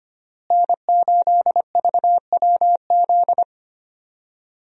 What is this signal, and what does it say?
Morse code, heard as a keyed tone, about 25 wpm, 700 Hz. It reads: N8VWZ
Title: USA Call Signs - 5 Characters - Repeated 3x - 25wpm